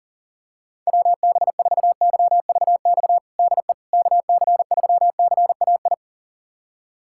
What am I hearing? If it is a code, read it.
Morse code, heard as a keyed tone, about 40 wpm, 690 Hz. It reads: WB4YVX DE KC3CAI
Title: USA Callsign DE Callsign - 6 Characters - 40wpm